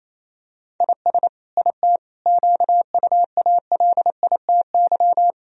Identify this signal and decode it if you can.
Morse code, heard as a keyed tone, about 28 wpm, 700 Hz. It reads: IS IT QUALITY
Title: Sentences from Top 1000 Words - Review Part 4 - 28wpm